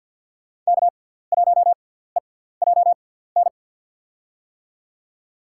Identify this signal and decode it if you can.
Morse code, heard as a keyed tone, about 47 wpm, spaced as for 17 wpm, 700 Hz. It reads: K1EJN
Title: USA Call Signs - 5 Characters - 6x Character Spacing - 50wpm